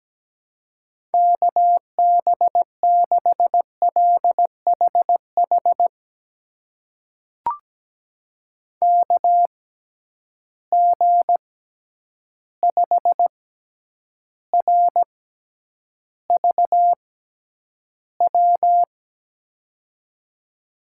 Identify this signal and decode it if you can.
Morse code, heard as a keyed tone, about 17 wpm, spaced as for 9 wpm, 700 Hz. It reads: KB6LHH KG5RVW
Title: USA Call Signs - 6 Characters - 6x Character Spacing - 17wpm